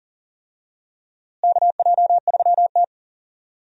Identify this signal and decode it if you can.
Morse code, heard as a keyed tone, about 40 wpm, 700 Hz. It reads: KJ3T
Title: USA Call Signs - 4 Characters - Repeated 3x - 40wpm